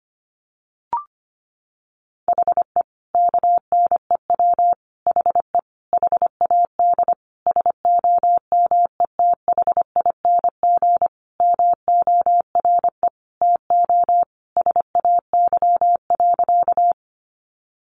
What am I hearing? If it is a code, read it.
Morse code, heard as a keyed tone, about 25 wpm, 700 Hz. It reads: HE KNEW HE HAD SOMETHING MORE TO SAY.
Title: Sentences from Top 600 Words 25wpm